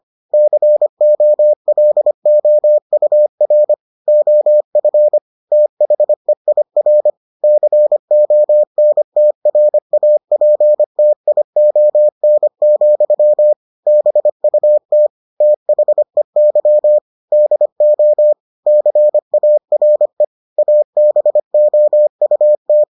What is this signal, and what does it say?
Morse code, heard as a keyed tone, about 25 wpm, 600 Hz. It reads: COLOUR OF THEIR CONTRAPTION, BUT THEY DO CARE ABOUT